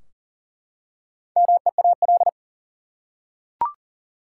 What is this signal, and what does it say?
Morse code, heard as a keyed tone, about 40 wpm, 700 Hz. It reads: MEAL
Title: Four Letter Words - 40wpm